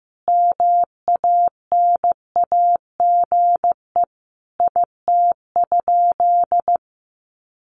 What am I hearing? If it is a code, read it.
Morse code, heard as a keyed tone, about 15 wpm, 700 Hz. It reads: MANAGE IT?
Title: Sentences from Top 1400 Words 15wpm